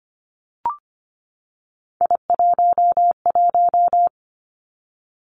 Morse code, heard as a keyed tone, about 25 wpm, 700 Hz. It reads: I11
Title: Three Challenging Letter-Number Combinations 25wpm